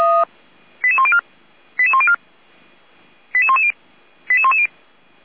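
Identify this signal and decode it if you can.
Five-tone selective call (ZVEI), heard as quick runs of five tones, tones of about 70 ms. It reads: 88104 88104 88110 88110
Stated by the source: Five-tone ZVEI selective call received on SDR.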